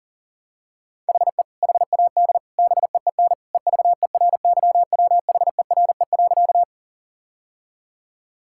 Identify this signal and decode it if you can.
Morse code, heard as a keyed tone, about 40 wpm, 700 Hz. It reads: HE HAD BEEN EVERYWHERE.
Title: Sentences from Top 1600 Words 40wpm